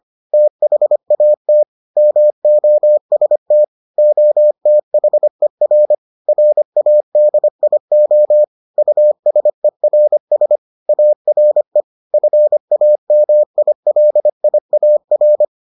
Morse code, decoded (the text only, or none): THAT MOST OTHER RADIO USERS ARE FAMILIAR